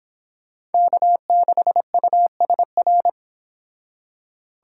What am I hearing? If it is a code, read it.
Morse code, heard as a keyed tone, about 31 wpm, 700 Hz. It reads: K6USR